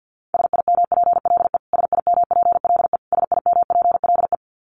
Morse code, heard as a keyed tone, about 49 wpm, 700 Hz. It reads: SIMPLE SIMPLE SIMPLE
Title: Top 500 Words Repeated 50wpm